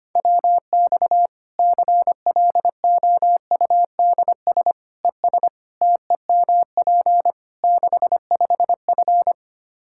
Morse code, heard as a keyed tone, about 25 wpm, 700 Hz. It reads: WX CLOUDS ES TEMP 65F
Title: Practice QSO - Exchange 2 - Info - 25wpm